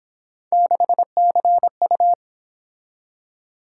Morse code, heard as a keyed tone, about 26 wpm, 700 Hz. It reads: 6CU
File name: Speed-Racing - Three Letter-Number Combinations - 22wpm